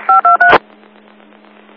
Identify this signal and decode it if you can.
DTMF, heard as telephone touch tones, tones of about 130 ms. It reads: 523